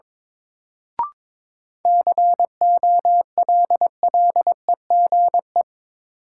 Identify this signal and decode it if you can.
Morse code, heard as a keyed tone, about 22 wpm, 700 Hz. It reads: COLLEGE
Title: Sentences from Top 1300 Words - New Word Drill Alt - 22wpm